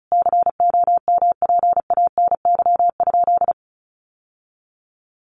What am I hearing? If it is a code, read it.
Morse code, heard as a keyed tone, about 34 wpm, 700 Hz. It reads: COMPANY?
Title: Sentences from Top 600 Words 35wpm